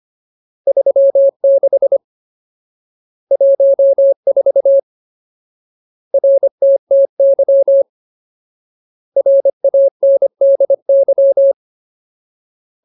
Morse code, heard as a keyed tone, about 25 wpm, 550 Hz. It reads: 36 14 RTTY RANDY
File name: Ultimate Word List / 25WPM Ultimate